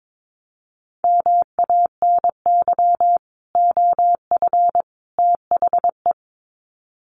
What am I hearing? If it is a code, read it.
Morse code, heard as a keyed tone, about 22 wpm, 700 Hz. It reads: MANY OF THE
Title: Sets of 3 Words 22wpm